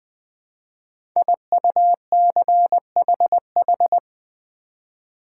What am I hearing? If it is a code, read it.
Morse code, heard as a keyed tone, about 20 wpm, 700 Hz. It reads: IUCHH
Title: Code Groups - 5 Letters - 20wpm